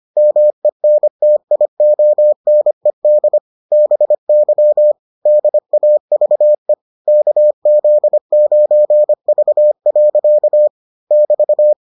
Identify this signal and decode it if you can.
Morse code, heard as a keyed tone, about 25 wpm, 600 Hz. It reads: MENTIONED BY DAVE KZ9V. <BT>